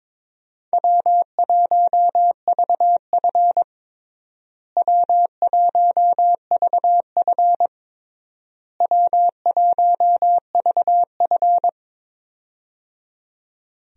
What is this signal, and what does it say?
Morse code, heard as a keyed tone, about 22 wpm, 700 Hz. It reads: W1VF W1VF W1VF
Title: USA Call Signs - 4 Characters - Repeated 3x - 22wpm